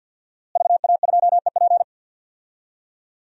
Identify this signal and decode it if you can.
Morse code, heard as a keyed tone, about 53 wpm, 700 Hz. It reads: VA2EP
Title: DXCC-100 Call Signs - 5 Characters - Repeated 2x - 50wpm